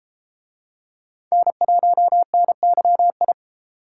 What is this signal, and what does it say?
Morse code, heard as a keyed tone, about 33 wpm, 700 Hz. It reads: N1NYI